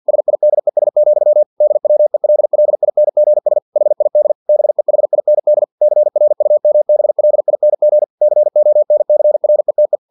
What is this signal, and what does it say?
Morse code, heard as a keyed tone, about 50 wpm, 600 Hz. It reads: SIDES, DWELLINGS HID BEHIND CRUMBLING CONCRETE
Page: Never 49wpm